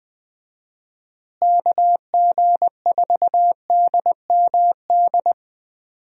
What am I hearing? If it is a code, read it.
Morse code, heard as a keyed tone, about 20 wpm, 700 Hz. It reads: KG4DMD
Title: USA Call Signs - 6 Characters - Repeated 3x - 20wpm